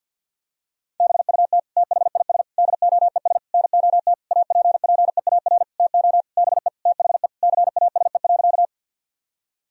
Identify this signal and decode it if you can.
Morse code, heard as a keyed tone, about 50 wpm, 700 Hz. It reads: BUT THIS DOES NOT APPEAR TO BE THE CASE.
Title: Sentences from Top 1000 Words - Review Part 2 - 50wpm